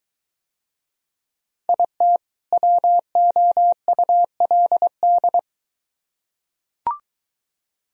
Morse code, heard as a keyed tone, about 22 wpm, 700 Hz. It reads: IT WOULD